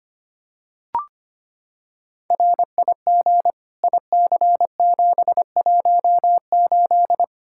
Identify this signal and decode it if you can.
Morse code, heard as a keyed tone, about 25 wpm, 700 Hz. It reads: RIG IC718
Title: Practice QSO - Exchange 2 - Info - 25wpm